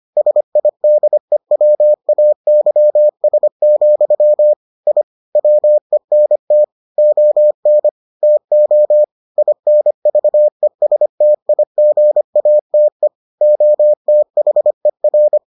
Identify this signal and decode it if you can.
Morse code, heard as a keyed tone, about 25 wpm, 600 Hz. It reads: SIDEWAYS, I WENT ON TO INVESTIGATE OTHER